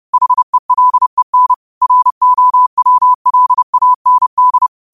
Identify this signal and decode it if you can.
Morse code, heard as a keyed tone, about 30 wpm, 1000 Hz. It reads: HELEN ROWLAND
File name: qotdfiles / 30wpm